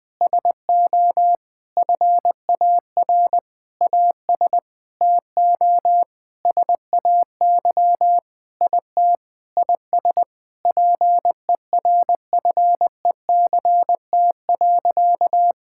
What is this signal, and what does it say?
Morse code, heard as a keyed tone, about 20 wpm, 700 Hz. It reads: SO FAR AS TO SAY IT IS PERFECT.
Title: Sentences with 8-Word Phrases - 20wpm